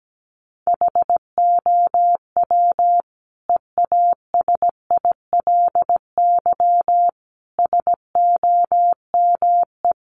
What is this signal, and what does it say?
Morse code, heard as a keyed tone, about 17 wpm, 700 Hz. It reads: HOW EASILY SOME
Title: Sentences from Top 900 Words 17wpm